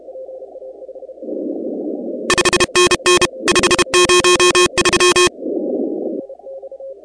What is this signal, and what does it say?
Morse code, heard as a keyed tone, about 32 wpm, 2500 Hz, with noise. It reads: 5NN 503
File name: contests / 2019-cw-wpx-kt5j-k5pi- -2330 0-16